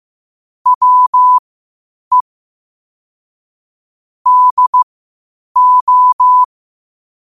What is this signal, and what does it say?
Morse code, heard as a keyed tone, about 15 wpm, spaced as for 8 wpm, 1000 Hz. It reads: WE DO
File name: qotdfiles / 8wpm